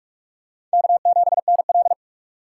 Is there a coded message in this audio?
Yes: Morse code from 0.7 to 1.9 s, heard as a keyed tone, about 42 wpm, 700 Hz. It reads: K7NL